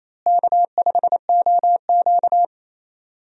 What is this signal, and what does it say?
Morse code, heard as a keyed tone, about 27 wpm, 700 Hz. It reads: K5OQ
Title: USA Call Signs - 4 Characters - Repeated 2x - 28wpm